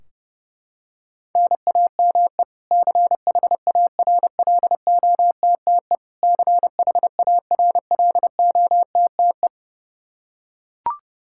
Morse code, heard as a keyed tone, about 32 wpm, 700 Hz. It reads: NAME CHARLOTTE CHARLOTTE
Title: Practice QSO - Exchange 1 - Beginning - 30wpm